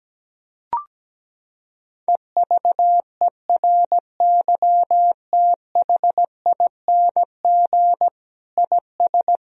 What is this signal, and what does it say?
Morse code, heard as a keyed tone, about 17 wpm, 700 Hz. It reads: EVERYTHING IS
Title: Sentences from Top 1000 Words - Review Part 2 - 17wpm